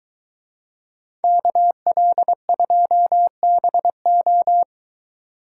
Morse code, heard as a keyed tone, about 22 wpm, 700 Hz. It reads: KL2BO